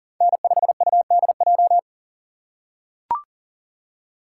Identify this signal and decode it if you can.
Morse code, heard as a keyed tone, about 37 wpm, 700 Hz. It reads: N5UDJ
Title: USA Call Signs - 5 Characters - 6x Character Spacing - 40wpm